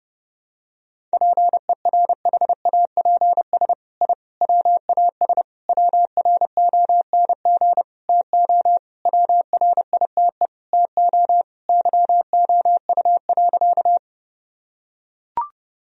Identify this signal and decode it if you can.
Morse code, heard as a keyed tone, about 32 wpm, 700 Hz. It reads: PERHAPS I WAS WRONG TO WRITE TO YOU.
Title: Sentences from Top 1000 Words - Review Part 1 - 30wpm